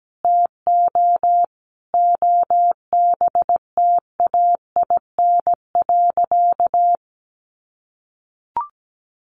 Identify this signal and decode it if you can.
Morse code, heard as a keyed tone, about 17 wpm, 700 Hz. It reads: TO OBTAIN.
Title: Sentences from Top 1300 Words 17wpm